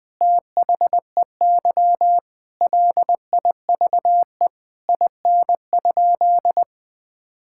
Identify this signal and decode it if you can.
Morse code, heard as a keyed tone, about 20 wpm, 700 Hz. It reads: THEY LIVE IN?
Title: Sentences from Top 1000 Words - Review Part 5 - 20wpm